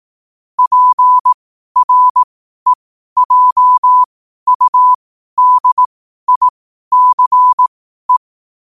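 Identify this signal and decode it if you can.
Morse code, heard as a keyed tone, about 18 wpm, spaced as for 12 wpm, 1000 Hz. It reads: PREJUDICE
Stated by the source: Morse code sent at 12 words per minute.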